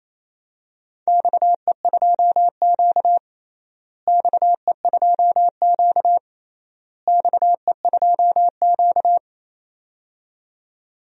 Morse code, heard as a keyed tone, about 28 wpm, 700 Hz. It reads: XE2Q XE2Q XE2Q
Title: DXCC-100 Call Signs - 4 Characters - Repeated 3x - 28wpm